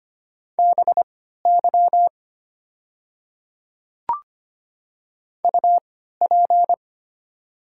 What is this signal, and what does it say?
Morse code, heard as a keyed tone, about 25 wpm, spaced as for 14 wpm, 700 Hz. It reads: BY UP
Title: 3x Character Spacing - Top 200 Words - 25wpm